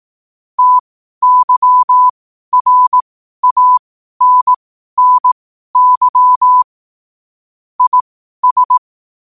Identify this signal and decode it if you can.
Morse code, heard as a keyed tone, about 18 wpm, spaced as for 12 wpm, 1000 Hz. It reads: TYRANNY IS
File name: 12wpm